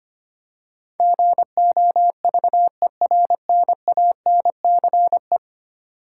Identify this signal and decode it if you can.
Morse code, heard as a keyed tone, about 25 wpm, 700 Hz. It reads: GOVERNANCE
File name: 10 Letter Words - First 3 Letters Given - 25wpm